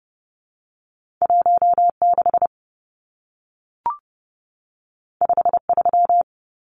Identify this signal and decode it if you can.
Morse code, heard as a keyed tone, about 30 wpm, 700 Hz. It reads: 16 53
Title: Numbers - 2-Digit 30wpm